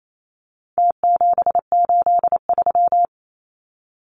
Morse code, heard as a keyed tone, about 28 wpm, 700 Hz. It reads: T783
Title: Numbers - Contest Style - 4-Digit 28wpm